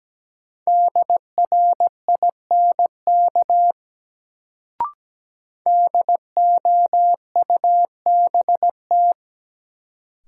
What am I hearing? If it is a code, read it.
Morse code, heard as a keyed tone, about 17 wpm, 700 Hz. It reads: DRINK DOUBT
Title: Sentences from Top 1500 Words - New Word Drill Alt - 17wpm